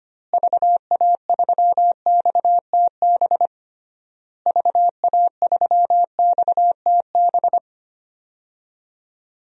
Morse code, heard as a keyed tone, about 25 wpm, 700 Hz. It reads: VA3XTB VA3XTB
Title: DXCC-100 Call Signs - 6 Characters - Repeated 3x - 25wpm